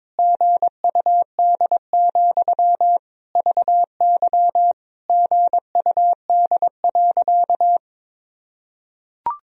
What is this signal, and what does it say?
Morse code, heard as a keyed tone, about 22 wpm, 700 Hz. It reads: GUD, VY GUD.
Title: Sentences from Top 200 Words and Abbreviations 22wpm